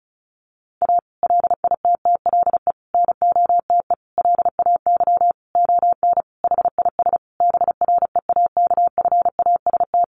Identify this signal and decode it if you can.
Morse code, heard as a keyed tone, about 34 wpm, 700 Hz. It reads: A LITTLE NOTE LAY ON HIS BREAKFAST
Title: Sentences from Top 1500 Words 35wpm